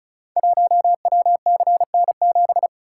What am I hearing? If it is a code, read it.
Morse code, heard as a keyed tone, about 36 wpm, 690 Hz. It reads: 1WCN7
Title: Code Groups - 5 Letter:Numbers - 35wpm